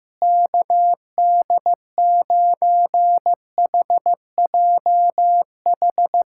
Morse code, heard as a keyed tone, about 15 wpm, 700 Hz. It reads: KD9HJH
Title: USA Callsign DE Callsign - 6 Characters - 15wpm